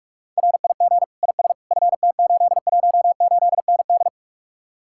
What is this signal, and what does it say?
Morse code, heard as a keyed tone, about 55 wpm, 700 Hz. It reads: RIG IS FT818ND
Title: Speed-Racing Common QSO Phrases - 45wpm